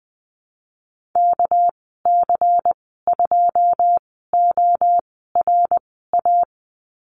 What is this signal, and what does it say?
Morse code, heard as a keyed tone, about 20 wpm, spaced as for 15 wpm, 700 Hz. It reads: KC2ORA
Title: 231 Lesson 37g - Letter x - Call Signs 20wpm